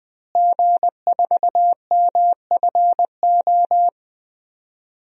Morse code, heard as a keyed tone, about 20 wpm, 700 Hz. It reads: G4MFO
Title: DXCC-100 Call Signs - 5 Characters - Repeated 2x - 20wpm